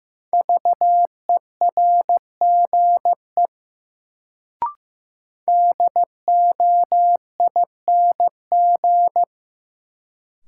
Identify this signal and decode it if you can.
Morse code, heard as a keyed tone, about 15 wpm, 700 Hz. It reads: VERGE DOING
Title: Sets of 4 Words - New Word Drill - 15wpm